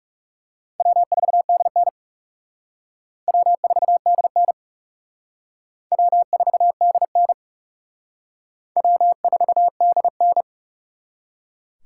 Morse code, heard as a keyed tone, about 35 wpm, 700 Hz. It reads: W4DN W4DN W4DN W4DN
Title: Speed Racing - USA 1x2 Call Signs 30wpm